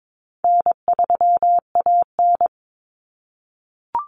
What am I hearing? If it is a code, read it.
Morse code, heard as a keyed tone, about 22 wpm, 700 Hz. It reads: N3AN
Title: Common Cut Numbers - 4-Digit 22wpm